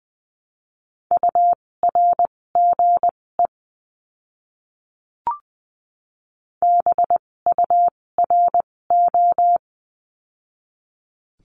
Morse code, heard as a keyed tone, about 20 wpm, spaced as for 16 wpm, 700 Hz. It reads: URGE BURO
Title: 256 Lesson 42a - Words 12-20wpm